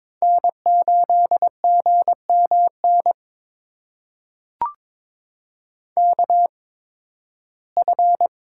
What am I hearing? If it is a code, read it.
Morse code, heard as a keyed tone, about 22 wpm, spaced as for 14 wpm, 700 Hz. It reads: N8GMN KF
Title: USA Call Signs - 5 Characters - 8x Character Spacing - 22wpm